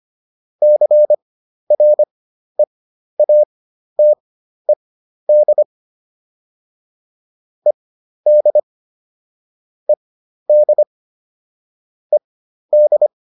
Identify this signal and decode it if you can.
Morse code, heard as a keyed tone, about 25 wpm, spaced as for 12 wpm, 600 Hz. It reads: CREATED ED ED ED